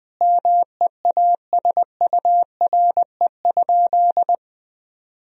Morse code, heard as a keyed tone, about 20 wpm, 700 Hz. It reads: MEASURE?
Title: Sentences from Top 1000 Words - Review Part 2 - 20wpm